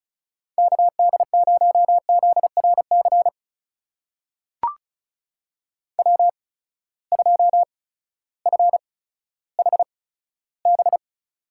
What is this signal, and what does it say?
Morse code, heard as a keyed tone, about 34 wpm, spaced as for 16 wpm, 700 Hz. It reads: KD0ZRC W2FHB
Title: USA Call Signs - 8x Character Spacing - 35wpm